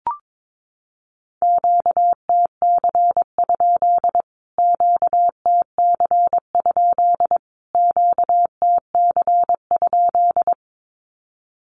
Morse code, heard as a keyed tone, about 22 wpm, 700 Hz. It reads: QTC? QTC? QTC?
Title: Q-Codes Repeated 3x 22wpm